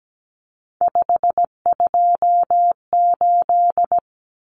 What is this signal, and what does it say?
Morse code, heard as a keyed tone, about 17 wpm, 700 Hz. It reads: 528
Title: Common Cut Numbers - 3-Digit 17wpm